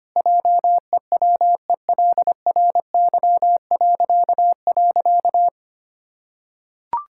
Morse code, heard as a keyed tone, about 25 wpm, 700 Hz. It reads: JEWELRY..
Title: Pangram Challenge v2 - 25wpm